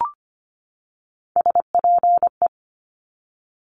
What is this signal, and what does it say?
Morse code, heard as a keyed tone, about 25 wpm, 700 Hz. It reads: SPE
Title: Top Three Letter Combinations in Words 25wpm